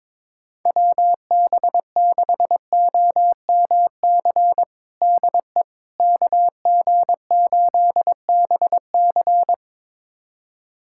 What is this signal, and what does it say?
Morse code, heard as a keyed tone, about 22 wpm, 700 Hz. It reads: WB6OMC DE KG8BC
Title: USA Callsign DE Callsign 22wpm